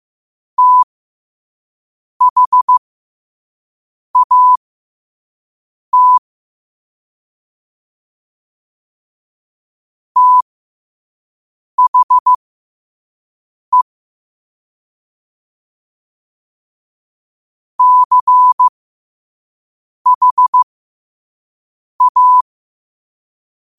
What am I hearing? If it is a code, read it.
Morse code, heard as a keyed tone, about 15 wpm, spaced as for 5 wpm, 1000 Hz. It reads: THAT THE CHA